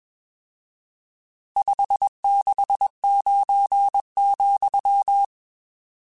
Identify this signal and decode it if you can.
Morse code, heard as a keyed tone, about 21 wpm, 800 Hz. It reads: 569,
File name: RSTs / RST-mp3s / Various RST @20